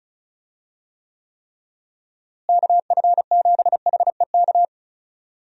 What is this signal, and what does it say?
Morse code, heard as a keyed tone, about 34 wpm, 700 Hz. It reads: KF7HEK
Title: USA Call Signs - 6 Characters - Repeated 3x - 35wpm